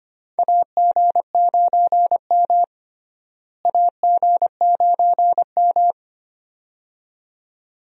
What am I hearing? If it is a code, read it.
Morse code, heard as a keyed tone, about 25 wpm, 700 Hz. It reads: AG9M AG9M
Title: USA Call Signs - 4 Characters - Repeated 2x - 25wpm